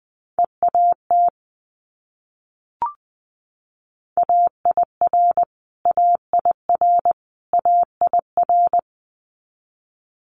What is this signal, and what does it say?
Morse code, heard as a keyed tone, about 21 wpm, 700 Hz. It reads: EAT AIR AIR AIR
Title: Multi-letter Phonemes Repeated 3x 20wpm